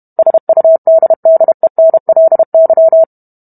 Morse code, heard as a keyed tone, about 32 wpm, 650 Hz. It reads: SUDDENLY